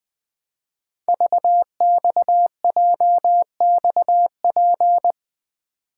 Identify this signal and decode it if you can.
Morse code, heard as a keyed tone, about 20 wpm, 700 Hz. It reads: VXJXP
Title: Code Groups - 5 Letter:Numbers - 20wpm